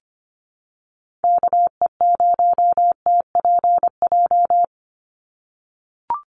Morse code, heard as a keyed tone, about 25 wpm, 700 Hz. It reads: KE0TPJ
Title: USA Call Signs 25wpm